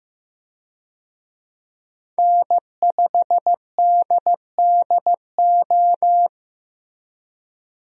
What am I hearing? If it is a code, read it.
Morse code, heard as a keyed tone, about 15 wpm, 700 Hz. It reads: N5DDO
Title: USA Call Signs - 5 Characters - Repeated 3x - 15wpm